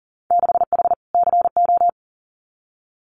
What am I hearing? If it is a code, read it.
Morse code, heard as a keyed tone, about 38 wpm, 700 Hz. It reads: 6H CO
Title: Winter Field Day Exchanges Encore 40wpm